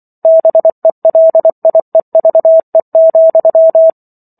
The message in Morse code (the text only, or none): BELIEVE,